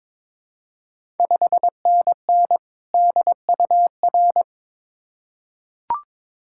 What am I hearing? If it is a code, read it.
Morse code, heard as a keyed tone, about 23 wpm, 690 Hz. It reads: 5NN DUR
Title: Ontario QSO Party Exchange 22wpm